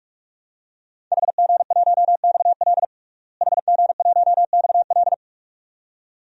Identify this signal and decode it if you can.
Morse code, heard as a keyed tone, about 41 wpm, 700 Hz. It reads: HG1XL HG1XL
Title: DXCC-100 Call Signs - 5 Characters - Repeated 2x - 45wpm